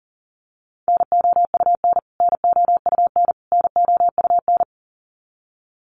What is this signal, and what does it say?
Morse code, heard as a keyed tone, about 40 wpm, 700 Hz. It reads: NOUN NOUN NOUN
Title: Top 500 Words Repeated 40wpm